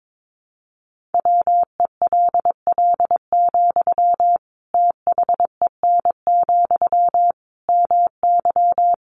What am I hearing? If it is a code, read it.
Morse code, heard as a keyed tone, about 22 wpm, 700 Hz. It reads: WELL, THEN, MY
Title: Sentences from Top 1400 Words 22wpm